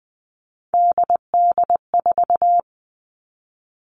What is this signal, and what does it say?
Morse code, heard as a keyed tone, about 20 wpm, 700 Hz. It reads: DD4
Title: Three Challenging Letter-Number Combinations 20wpm